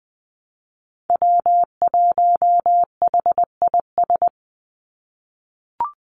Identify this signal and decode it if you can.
Morse code, heard as a keyed tone, about 20 wpm, 700 Hz. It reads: W1HIS
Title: Call Signs - 20wpm